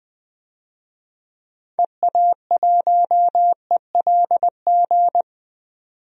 Morse code, heard as a keyed tone, about 20 wpm, 700 Hz. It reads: EA1ELG
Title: DXCC-100 Call Signs - 6 Characters - 20wpm